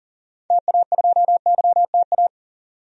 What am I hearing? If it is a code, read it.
Morse code, heard as a keyed tone, about 40 wpm, 700 Hz. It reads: TA2YTA
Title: DXCC-100 Call Signs - 6 Characters - 40wpm